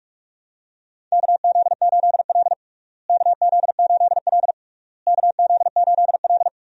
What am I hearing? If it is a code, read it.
Morse code, heard as a keyed tone, about 47 wpm, 700 Hz. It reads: KZ8L KZ8L KZ8L
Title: USA Call Signs - 4 Characters - Repeated 3x - 45wpm